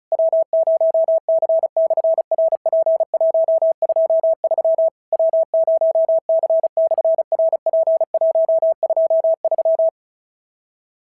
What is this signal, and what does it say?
Morse code, heard as a keyed tone, about 36 wpm, 650 Hz. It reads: W0C/RP123 W0C/RP123
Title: SOTA QSO Practice 35wpm